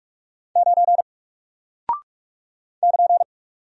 Morse code, heard as a keyed tone, about 44 wpm, 700 Hz. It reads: 9 <KN>
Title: Single Character (Letter:Number:Punctuation:Prosign) 45wpm